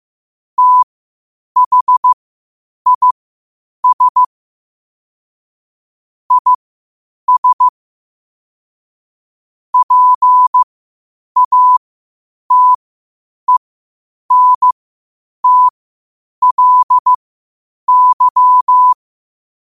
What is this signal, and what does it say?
Morse code, heard as a keyed tone, about 15 wpm, spaced as for 8 wpm, 1000 Hz. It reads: THIS IS PATENTLY
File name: qotdfiles / 8wpm